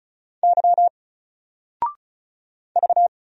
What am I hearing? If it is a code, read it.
Morse code, heard as a keyed tone, about 36 wpm, 700 Hz. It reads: Y V
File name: Single-Letter-Number-35wpm